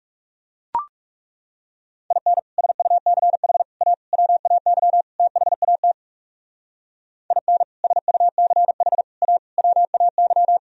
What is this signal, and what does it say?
Morse code, heard as a keyed tone, about 36 wpm, 700 Hz. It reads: IN SUCH A WAY THAT IN SUCH A WAY
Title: Speed-Racing - Sets of 5 Words 30wpm